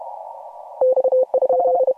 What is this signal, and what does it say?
Morse code, heard as a keyed tone, about 31 wpm, 520 Hz. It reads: X?
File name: AUDIO2018 / HG1G CW